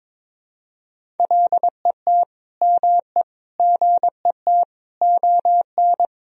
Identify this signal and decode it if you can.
Morse code, heard as a keyed tone, about 21 wpm, 700 Hz. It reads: LET ME GET ON
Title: Sentences from Top 300 Words and Abbreviations 22wpm